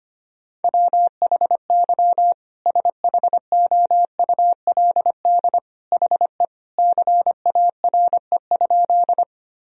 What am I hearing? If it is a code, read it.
Morse code, heard as a keyed tone, about 25 wpm, 700 Hz. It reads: WHY SHOULD HE CARE?
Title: Sentences from Top 1000 Words - Review Part 1 - 25wpm